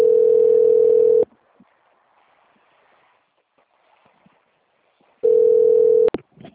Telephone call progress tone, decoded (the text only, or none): ringback tone